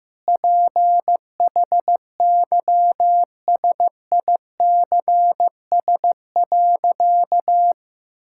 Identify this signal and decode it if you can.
Morse code, heard as a keyed tone, about 15 wpm, 700 Hz. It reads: PHYSICS.
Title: Sentences from Top 1000 Words - Review Part 5 - 15wpm